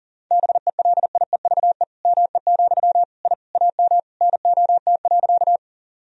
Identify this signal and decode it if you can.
Morse code, heard as a keyed tone, about 40 wpm, 700 Hz. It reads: BELIEVE ME, I AM NOT.
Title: Sentences from Top 1000 Words - Review Part 2 - 40wpm